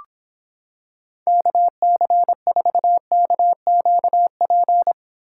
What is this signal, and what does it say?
Morse code, heard as a keyed tone, about 26 wpm, 700 Hz. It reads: KC4KQP